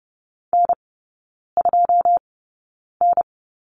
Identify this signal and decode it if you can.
Morse code, heard as a keyed tone, about 30 wpm, spaced as for 9 wpm, 700 Hz. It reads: N2N
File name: Lesson 18e - Letter m - Three Characters 30wpm